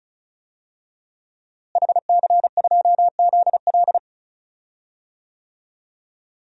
Morse code, heard as a keyed tone, about 34 wpm, 700 Hz. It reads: HC2ZL
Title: DXCC-100 Call Signs - 5 Characters - Repeated 2x - 35wpm